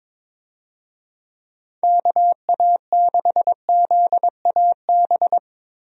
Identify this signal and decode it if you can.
Morse code, heard as a keyed tone, about 22 wpm, 700 Hz. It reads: KA6ZAB
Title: USA Call Signs - 6 Characters - Repeated 2x - 22wpm